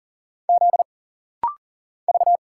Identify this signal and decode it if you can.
Morse code, heard as a keyed tone, about 40 wpm, 700 Hz. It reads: Z V
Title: Single Letters - Rapid-Fire 40wpm